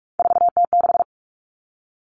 Morse code, heard as a keyed tone, about 44 wpm, 700 Hz. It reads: 4T6
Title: Common Cut Numbers - 3-Digit 45wpm